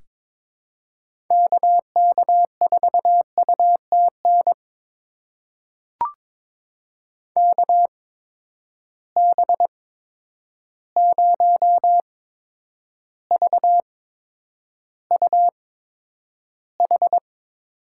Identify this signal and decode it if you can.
Morse code, heard as a keyed tone, about 22 wpm, spaced as for 9 wpm, 700 Hz. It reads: KK4UTN KB0VUH
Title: USA Call Signs - 6 Characters - 8x Character Spacing - 22wpm